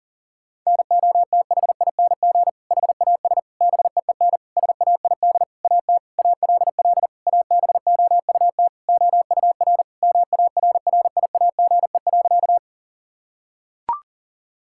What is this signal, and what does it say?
Morse code, heard as a keyed tone, about 40 wpm, 700 Hz. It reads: NOTHING HAS BEEN SAID AT ALL ABOUT OUR MARRIAGE.
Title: Sentences from Top 1000 Words - Review Part 5 - 40wpm